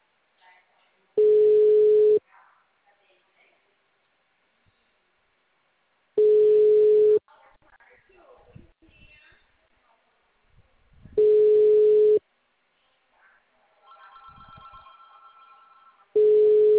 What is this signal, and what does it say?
Ringback tone (Europe, 425 Hz), with speech.